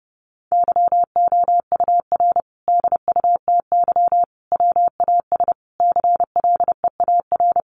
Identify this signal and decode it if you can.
Morse code, heard as a keyed tone, about 30 wpm, 700 Hz. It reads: YOUR DUTY WAS CLEAR
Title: Sentences from Top 1500 Words 30wpm